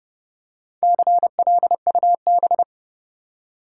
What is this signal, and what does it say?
Morse code, heard as a keyed tone, about 30 wpm, 700 Hz. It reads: CLUB
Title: Four Letter Words - 30wpm